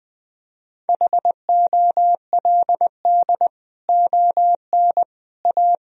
Morse code, heard as a keyed tone, about 20 wpm, 700 Hz. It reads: HOLD ON A
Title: Sentences from Top 1000 Words - Review Part 2 - 20wpm